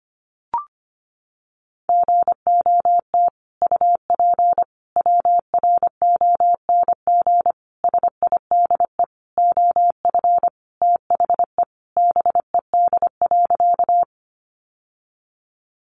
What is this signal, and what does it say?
Morse code, heard as a keyed tone, about 25 wpm, 700 Hz. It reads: GOT UP WRONG SIDE OF THE BED.
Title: Sentences from Top 900 Words 25wpm